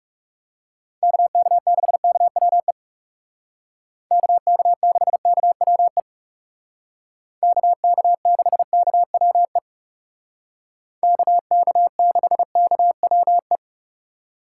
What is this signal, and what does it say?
Morse code, heard as a keyed tone, about 37 wpm, 700 Hz. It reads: KK6KWE KK6KWE KK6KWE KK6KWE
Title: Speed Racing - USA 2x3 Call Signs 30wpm